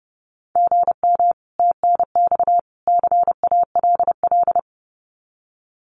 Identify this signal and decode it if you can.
Morse code, heard as a keyed tone, about 30 wpm, 700 Hz. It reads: GM TNX CALL
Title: Common QSO Phrases 30wpm